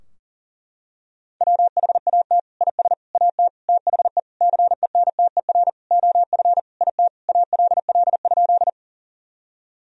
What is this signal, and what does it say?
Morse code, heard as a keyed tone, about 40 wpm, 700 Hz. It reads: WHAT IS AT THE CENTER OF IT ALL?
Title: Sentences from Top 1000 Words - Review Part 2 - 40wpm